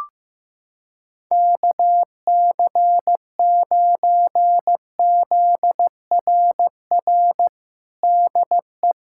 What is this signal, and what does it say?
Morse code, heard as a keyed tone, about 15 wpm, 700 Hz. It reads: KC9ZRR DE
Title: USA Callsign DE Callsign 15wpm